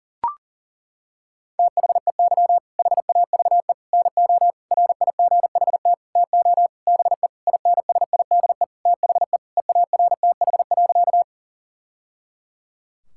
Morse code, heard as a keyed tone, about 40 wpm, 700 Hz. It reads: THEY HAVE NO RIGHT TO BE INSIDE THE EARTH.
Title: Sentences from Top 1000 Words - Review Part 6 - 40wpm